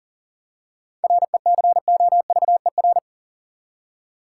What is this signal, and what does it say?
Morse code, heard as a keyed tone, about 40 wpm, 700 Hz. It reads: RECOVER
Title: Seven Letter Words - 40wpm